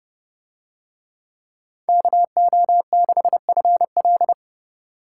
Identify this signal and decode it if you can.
Morse code, heard as a keyed tone, about 29 wpm, 700 Hz. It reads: KO6FL